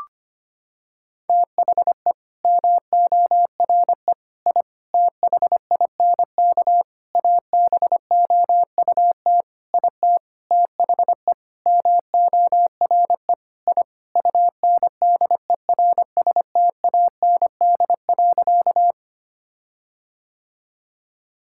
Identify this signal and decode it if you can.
Morse code, heard as a keyed tone, about 25 wpm, 700 Hz. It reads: THE MORE I THINK ABOUT IT THE MORE I UNDERSTAND.
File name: practice / 25wpm / Sentences with 9-Word Phrases - 25wpm